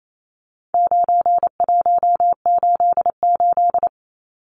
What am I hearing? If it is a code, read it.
Morse code, heard as a keyed tone, about 28 wpm, 700 Hz. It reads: 9188
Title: Numbers - 4-Digit 28wpm